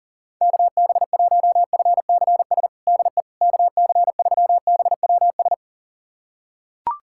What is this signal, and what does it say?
Morse code, heard as a keyed tone, about 40 wpm, 700 Hz. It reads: KB1FCS DE KC3BWS
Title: USA Callsign DE Callsign 40wpm